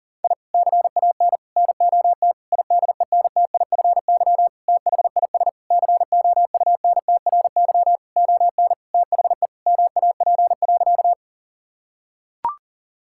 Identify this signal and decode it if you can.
Morse code, heard as a keyed tone, about 40 wpm, 700 Hz. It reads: I CAN NOT IDENTIFY THIS COUNTRY ON THE MAP.
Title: Sentences from Top 1000 Words - Review Part 6 - 40wpm